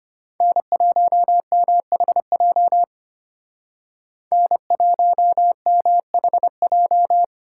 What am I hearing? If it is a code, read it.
Morse code, heard as a keyed tone, about 33 wpm, 700 Hz. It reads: N1MHJ N1MHJ